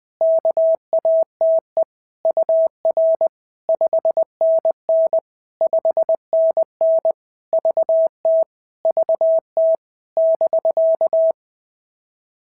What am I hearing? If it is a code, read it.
Morse code, heard as a keyed tone, about 20 wpm, 650 Hz. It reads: KATE UR 5NN 5NN VT VT <BK>
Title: POTA QSO Practice - 20wpm